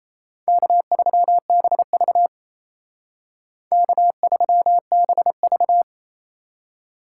Morse code, heard as a keyed tone, about 35 wpm, 700 Hz. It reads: K3BV K3BV
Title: Speed Racing - USA 1x2 Call Signs 28wpm